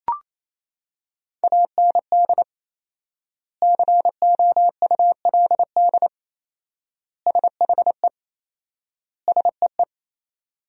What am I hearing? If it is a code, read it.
Morse code, heard as a keyed tone, about 27 wpm, 700 Hz. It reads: AND COULD SHE SEE
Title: 4x Word-Spacing - Sentences from Top 100 Words - 28wpm